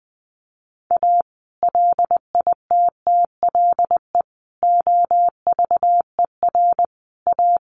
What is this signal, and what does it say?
Morse code, heard as a keyed tone, about 20 wpm, 700 Hz. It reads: A LITTLE OVER A
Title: Sentences from Top 1300 Words 20wpm